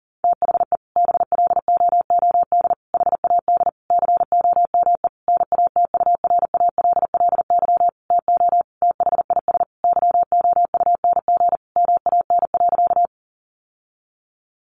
Morse code, heard as a keyed tone, about 37 wpm, 700 Hz. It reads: THE BLOOD HAD COME NATURALLY TO THIS YOUNG MAN.
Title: Sentences from Top 1300 Words 40wpm